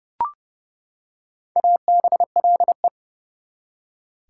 Morse code, heard as a keyed tone, about 29 wpm, 700 Hz. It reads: ABLE
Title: Prefixes and Suffixes 30wpm